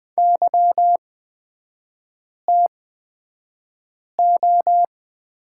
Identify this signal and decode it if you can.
Morse code, heard as a keyed tone, about 20 wpm, spaced as for 5 wpm, 700 Hz. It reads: YTO
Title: 250 Lesson 40e - BK - Three Characters 20wpm